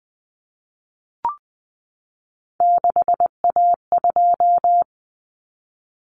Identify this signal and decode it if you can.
Morse code, heard as a keyed tone, about 20 wpm, 700 Hz. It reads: 6A2
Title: Common Cut Numbers - 3-Digit 20wpm